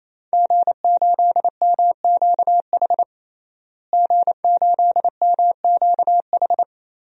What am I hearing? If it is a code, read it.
Morse code, heard as a keyed tone, about 27 wpm, 700 Hz. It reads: G8MQH G8MQH
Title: DXCC-100 Call Signs - 5 Characters - Repeated 3x - 28wpm